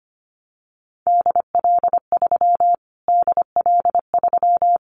Morse code, heard as a keyed tone, about 25 wpm, 700 Hz. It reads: DL3 DL3
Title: Callsign Prefixes for Top 25 DXCC - Repeated 3x - 25wpm